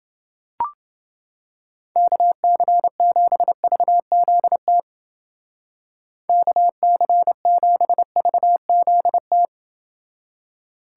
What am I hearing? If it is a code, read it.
Morse code, heard as a keyed tone, about 29 wpm, 700 Hz. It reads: KC7VZT KC7VZT